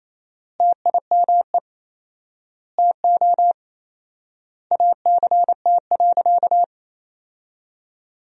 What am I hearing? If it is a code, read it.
Morse code, heard as a keyed tone, about 28 wpm, 700 Hz. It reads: TIME TO ACT.
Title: 4x Word-Spacing - Sentences from Top 200 Words - 28wpm